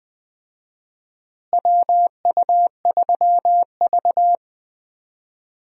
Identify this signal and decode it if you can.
Morse code, heard as a keyed tone, about 21 wpm, 700 Hz. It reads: WU3V
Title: USA Call Signs - 4 Characters - Repeated 3x - 20wpm